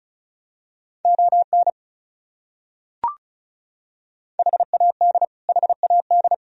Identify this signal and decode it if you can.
Morse code, heard as a keyed tone, about 36 wpm, 700 Hz. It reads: ON HAD HAD
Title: Top 100 Words Repeated 35wpm